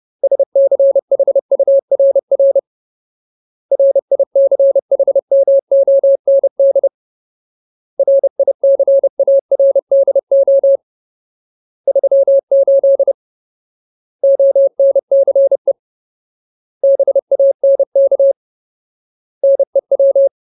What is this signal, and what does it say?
Morse code, heard as a keyed tone, about 30 wpm, 550 Hz. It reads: SCHURR RICHMOND RICARDO 38 ONCE BANK NEW
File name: Ultimate Word List / 30WPM Ultimate